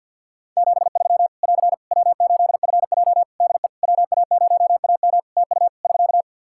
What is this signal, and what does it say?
Morse code, heard as a keyed tone, about 48 wpm, 700 Hz. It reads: 73 <AR> W8FJ DE WA0AM TU <SK>
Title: Practice QSO - Exchange 4 - Ending - 50wpm